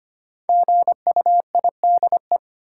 Morse code, heard as a keyed tone, about 25 wpm, 700 Hz. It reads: GUIDE
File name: 5 Letter Words - First 3 Letters Given - 25wpm